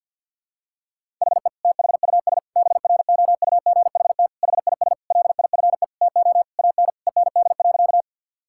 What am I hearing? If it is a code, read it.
Morse code, heard as a keyed tone, about 48 wpm, 700 Hz. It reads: HE THUS BROUGHT HIS LIFE TO AN END.
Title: Sentences from Top 1000 Words - Review Part 5 - 50wpm